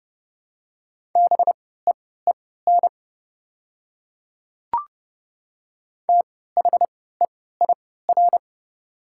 Morse code, heard as a keyed tone, about 30 wpm, spaced as for 17 wpm, 700 Hz. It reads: BEEN THEIR
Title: 3x Character Spacing - Top 200 Words - 30wpm